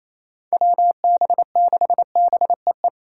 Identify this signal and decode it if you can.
Morse code, heard as a keyed tone, about 28 wpm, 700 Hz. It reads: WB6BEE
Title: K1USN SST QSO Practice - 28wpm